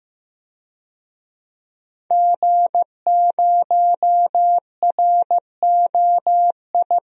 Morse code, heard as a keyed tone, about 15 wpm, 700 Hz. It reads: G0ROI
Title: DXCC-100 Call Signs - 5 Characters - 15wpm